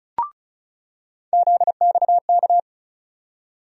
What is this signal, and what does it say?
Morse code, heard as a keyed tone, about 36 wpm, 700 Hz. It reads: ZXK
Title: USA Call Sign Postfixes - 3 Letters 35wpm